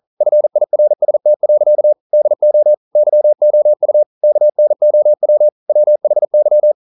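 Morse code, heard as a keyed tone, about 42 wpm, 600 Hz. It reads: FIRST. DO YOU KNOW WHY
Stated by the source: Morse code at 40 wpm, practice text, CW med Gnister.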